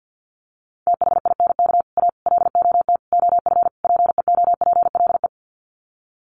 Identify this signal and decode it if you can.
Morse code, heard as a keyed tone, about 50 wpm, 700 Hz. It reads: THINK A LOT OF PEOPLE
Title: Sets of 5 Words 50wpm